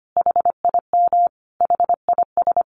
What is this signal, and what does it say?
Morse code, heard as a keyed tone, about 25 wpm, 700 Hz. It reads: HIM HIS
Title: Sentences from Top 1500 Words 25wpm